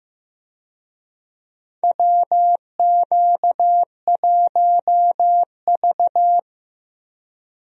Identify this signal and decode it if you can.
Morse code, heard as a keyed tone, about 15 wpm, 700 Hz. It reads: WQ1V
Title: USA Call Signs - 4 Characters - Repeated 3x - 15wpm